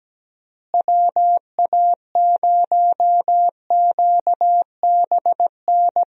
Morse code, heard as a keyed tone, about 17 wpm, 700 Hz. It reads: WA0QBN
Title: USA Call Signs - 6 Characters - Repeated 3x - 17wpm